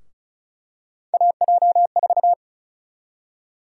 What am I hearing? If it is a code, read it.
Morse code, heard as a keyed tone, about 36 wpm, 700 Hz. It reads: AJ4
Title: USA Call Sign Prefixes - 35wpm